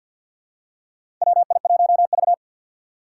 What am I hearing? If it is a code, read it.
Morse code, heard as a keyed tone, about 52 wpm, 700 Hz. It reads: WI1V
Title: USA Call Signs - 4 Characters - 3x Character Spacing - 50wpm